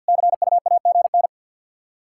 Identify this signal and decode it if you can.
Morse code, heard as a keyed tone, about 52 wpm, 700 Hz. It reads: CUAGN
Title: Practice QSO - Exchange 4 - Ending - 50wpm